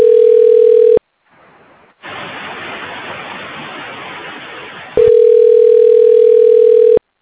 Ringback tone (North America), 440 + 480 Hz.